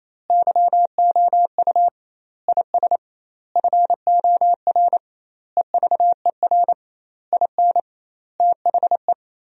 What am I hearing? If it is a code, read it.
Morse code, heard as a keyed tone, about 28 wpm, 700 Hz. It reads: YOU IS FOR EVER IN THE
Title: 2x Word-Spacing - Sentences from Top 400 Words - 28wpm